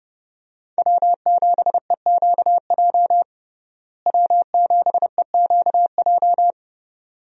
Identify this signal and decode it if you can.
Morse code, heard as a keyed tone, about 32 wpm, 700 Hz. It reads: W7EQJ W7EQJ
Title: USA Call Signs - 5 Characters - Repeated 2x - 30wpm